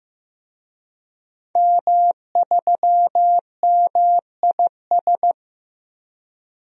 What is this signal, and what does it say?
Morse code, heard as a keyed tone, about 15 wpm, 700 Hz. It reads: M3MIS
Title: DXCC-100 Call Signs - 5 Characters - 15wpm